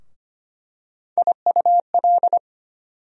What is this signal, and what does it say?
Morse code, heard as a keyed tone, about 25 wpm, 700 Hz. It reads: IUL
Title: USA Call Sign Postfixes - 3 Letters 25wpm